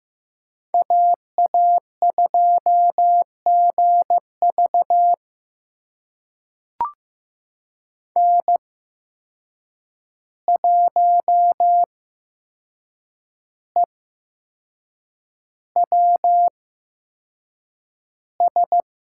Morse code, heard as a keyed tone, about 15 wpm, spaced as for 6 wpm, 700 Hz. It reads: AA2GV N1EWS
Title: USA Call Signs - 5 Characters - 8x Character Spacing - 15wpm